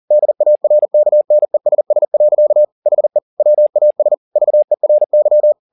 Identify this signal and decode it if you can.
Morse code, heard as a keyed tone, about 40 wpm, 600 Hz. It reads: DARKNESS. HE WAS VERY